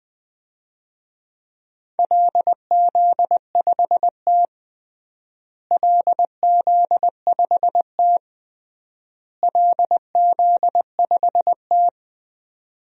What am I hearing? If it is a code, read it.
Morse code, heard as a keyed tone, about 20 wpm, 700 Hz. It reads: LZ5T LZ5T LZ5T
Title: DXCC-100 Call Signs - 4 Characters - Repeated 3x - 20wpm